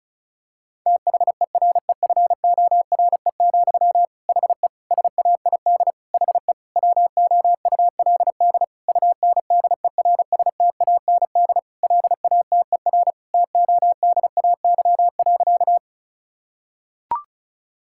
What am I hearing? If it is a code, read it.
Morse code, heard as a keyed tone, about 36 wpm, 700 Hz. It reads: THEREFORE, HE SAID HE WOULD UNDERSTAND LATER TODAY.
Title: Sentences from Top 1000 Words - Review Part 5 - 35wpm